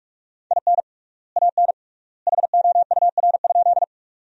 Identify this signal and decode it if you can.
Morse code, heard as a keyed tone, about 48 wpm, 700 Hz. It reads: IN AN HOUR?
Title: 3x Word-Spacing - Sentences from Top 400 Words - 45wpm